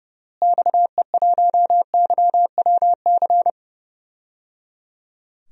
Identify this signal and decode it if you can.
Morse code, heard as a keyed tone, about 30 wpm, 700 Hz. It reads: XE1YWC
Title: DXCC-100 Call Signs - 6 Characters - Repeated 3x - 30wpm